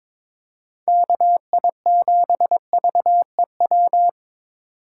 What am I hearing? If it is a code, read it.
Morse code, heard as a keyed tone, about 22 wpm, 700 Hz. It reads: KI7VEW
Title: USA Call Signs - 6 Characters - 4x Character Spacing - 22wpm